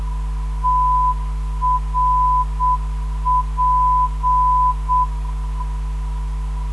Morse code, heard as a keyed tone, about 7 wpm, 1000 Hz. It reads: TRP
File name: ndb zvok / TRP - 317.5kHz